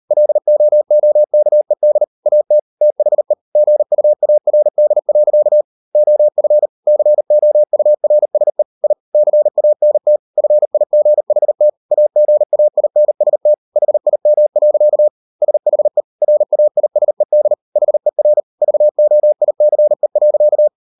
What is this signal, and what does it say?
Morse code, heard as a keyed tone, about 38 wpm, 600 Hz. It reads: LOOKED AT THE GUARD. OF COURSE I CANT FIGHT AGAINST HIM. SHE RAISED HER VOICE.